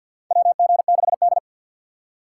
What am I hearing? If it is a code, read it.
Morse code, heard as a keyed tone, about 50 wpm, 700 Hz. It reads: WG6D
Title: USA Call Signs - 4 Characters - 2x Character Spacing - 50wpm